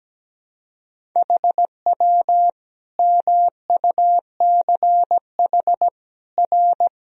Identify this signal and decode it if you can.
Morse code, heard as a keyed tone, about 17 wpm, 700 Hz. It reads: HW MUCH R
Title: Sentences from Top 200 Words and Abbreviations 17wpm